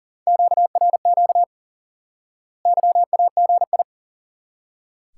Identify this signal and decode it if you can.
Morse code, heard as a keyed tone, about 40 wpm, 700 Hz. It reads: QRQ YAGI
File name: IQE Course 3 - Lesson 89 - YAGI - Two Elements - 40wpm